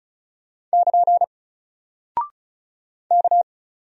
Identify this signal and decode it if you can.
Morse code, heard as a keyed tone, about 36 wpm, 700 Hz. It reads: <KN> K
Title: Single Character (Letter:Number:Punctuation:Prosign) Alt 35wpm